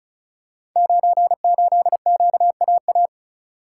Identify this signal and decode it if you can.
Morse code, heard as a keyed tone, about 36 wpm, 700 Hz. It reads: 98QAA
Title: Code Groups - 5 Letter:Numbers - 35wpm